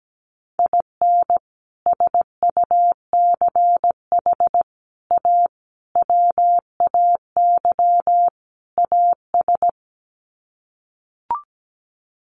Morse code, heard as a keyed tone, about 17 wpm, 700 Hz. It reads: IN SUCH A WAY AS
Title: Sets of 5 Words Encore 17wpm